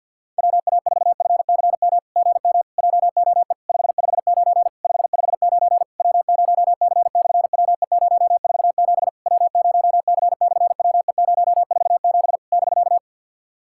Morse code, heard as a keyed tone, about 50 wpm, 700 Hz. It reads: WA4FCM GM JOE 559 559 W0C/WE047 W0C/WE047 <BK>
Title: SOTA QSO Practice 50wpm